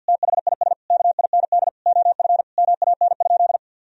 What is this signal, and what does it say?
Morse code, heard as a keyed tone, about 50 wpm, 700 Hz. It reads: THIS KIND OF MAN?
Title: Sentences from Top 1000 Words - Review Part 4 - 50wpm